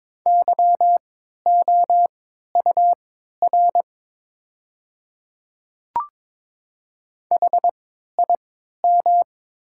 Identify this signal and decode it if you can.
Morse code, heard as a keyed tone, about 22 wpm, spaced as for 12 wpm, 690 Hz. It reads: YOUR HIM
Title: 3x Character Spacing - Top 100 Words - 22wpm